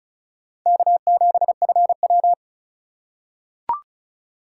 Morse code, heard as a keyed tone, about 34 wpm, 700 Hz. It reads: K7FW
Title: USA Call Signs - 4 Characters - 4x Character Spacing - 35wpm